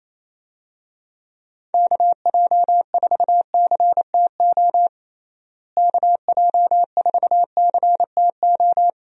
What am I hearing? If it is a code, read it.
Morse code, heard as a keyed tone, about 28 wpm, 700 Hz. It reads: KJ4CTO KJ4CTO
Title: USA Call Signs - 6 Characters - Repeated 2x - 28wpm